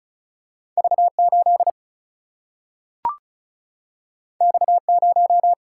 Morse code, heard as a keyed tone, about 36 wpm, 700 Hz. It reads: V 8 X 0
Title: Single-Letter Single-Number 35wpm